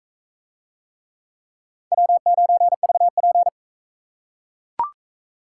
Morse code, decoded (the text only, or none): W9VP